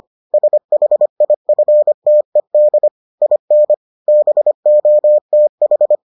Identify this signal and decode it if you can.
Morse code, heard as a keyed tone, about 25 wpm, 600 Hz. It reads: SHIFTED IN BOTH